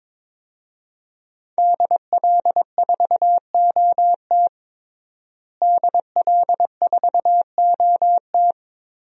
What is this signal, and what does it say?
Morse code, heard as a keyed tone, about 22 wpm, 700 Hz. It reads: DL4OT DL4OT
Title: USA Call Signs - 5 Characters - Repeated 3x - 22wpm